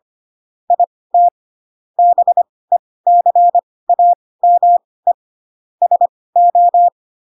Morse code, heard as a keyed tone, about 25 wpm, spaced as for 18 wpm, 700 Hz. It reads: IT BECAME SO
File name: Pinocchio ch3 12 25 ik1hge